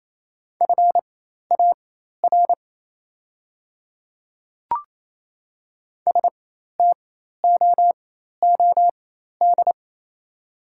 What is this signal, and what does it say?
Morse code, heard as a keyed tone, about 27 wpm, spaced as for 13 wpm, 690 Hz. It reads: FAR STOOD
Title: 4x Character Spacing - Top 500 Words - 28wpm